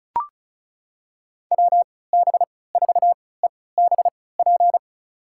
Morse code, heard as a keyed tone, about 36 wpm, spaced as for 20 wpm, 700 Hz. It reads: WB4EBP
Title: USA Call Signs - 6 Characters - 3x Character Spacing - 35wpm